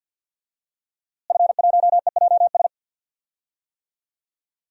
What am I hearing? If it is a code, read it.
Morse code, heard as a keyed tone, about 52 wpm, 700 Hz. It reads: F1EJS
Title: DXCC-100 Call Signs - 5 Characters - Repeated 3x - 50wpm